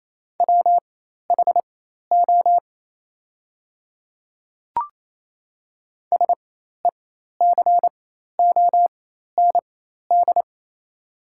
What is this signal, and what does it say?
Morse code, heard as a keyed tone, about 28 wpm, spaced as for 13 wpm, 690 Hz. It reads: WHO SECOND
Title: 4x Character Spacing - Top 300 Words - 28wpm